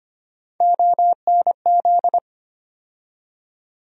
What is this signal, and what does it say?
Morse code, heard as a keyed tone, about 25 wpm, 700 Hz. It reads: ONZ
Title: USA Call Sign Postfixes - 3 Letters 25wpm